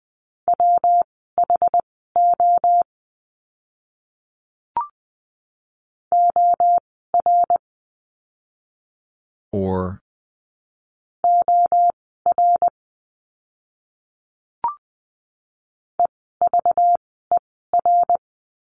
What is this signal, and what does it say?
Morse code, heard as a keyed tone, about 20 wpm, spaced as for 14 wpm, 700 Hz. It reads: WHO OR OR EVER
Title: 192 Lesson 31f - Letter v - Words 20wpm